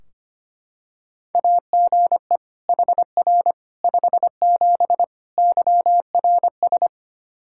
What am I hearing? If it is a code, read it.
Morse code, heard as a keyed tone, about 25 wpm, 700 Hz. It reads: AGE HR 57 YRS
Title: K1USN SST QSO Practice 25wpm